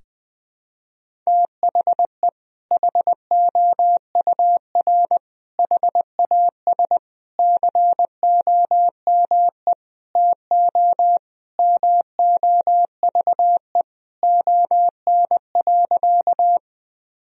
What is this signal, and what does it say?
Morse code, heard as a keyed tone, about 20 wpm, 700 Hz. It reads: THE HOUR HAS COME TO MOVE ON.
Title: Sentences from Top 1000 Words - Review Part 1 - 20wpm